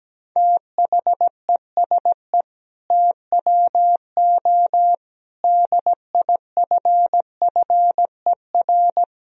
Morse code, heard as a keyed tone, about 17 wpm, 700 Hz. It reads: THESE TWO DIFFER
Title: Sentences from Top 1000 Words - Review Part 1 - 17wpm